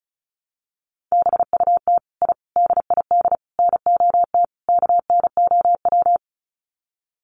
Morse code, heard as a keyed tone, about 34 wpm, 700 Hz. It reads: BUT I DID NOT KNOW
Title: Sets of 5 Words Encore 35wpm